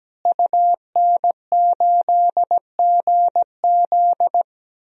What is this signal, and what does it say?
Morse code, heard as a keyed tone, about 17 wpm, 700 Hz. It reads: UN8GZ
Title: DXCC-100 Call Signs - 5 Characters - Repeated 3x - 17wpm